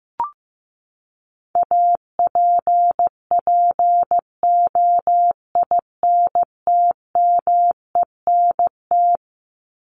Morse code, heard as a keyed tone, about 15 wpm, 700 Hz. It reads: APPOINTMENT
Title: Top 1751-2000 Words 15wpm